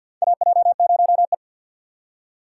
Morse code, heard as a keyed tone, about 49 wpm, 700 Hz. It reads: AJ0E
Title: USA Call Signs - 4 Characters - 6x Character Spacing - 50wpm